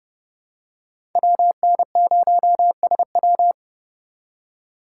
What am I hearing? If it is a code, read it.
Morse code, heard as a keyed tone, about 32 wpm, 700 Hz. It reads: WN0SW
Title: USA Call Signs - 5 Characters - 3x Character Spacing - 30wpm